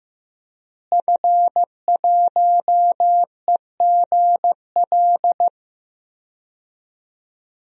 Morse code, heard as a keyed tone, about 15 wpm, 700 Hz. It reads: F1EGL
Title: DXCC-100 Call Signs - 5 Characters - Repeated 2x - 15wpm